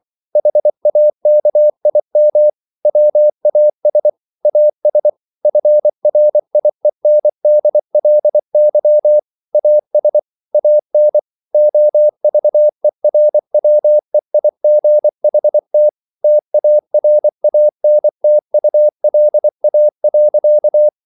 Morse code, heard as a keyed tone, about 24 wpm, 600 Hz. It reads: HAKIM WAS AS FRIENDLY AS AN OVERWEIGHT TARANTULA.